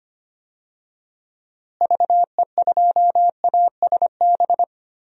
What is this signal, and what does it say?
Morse code, heard as a keyed tone, about 25 wpm, 700 Hz. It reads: VE2ASB
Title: DXCC-100 Call Signs - 6 Characters - Repeated 3x - 25wpm